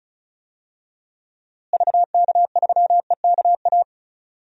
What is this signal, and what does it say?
Morse code, heard as a keyed tone, about 36 wpm, 700 Hz. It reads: VK3EKA
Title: DXCC-100 Call Signs - 6 Characters - Repeated 2x - 35wpm